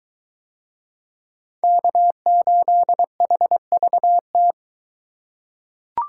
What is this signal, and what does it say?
Morse code, heard as a keyed tone, about 23 wpm, 700 Hz. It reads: K8HVT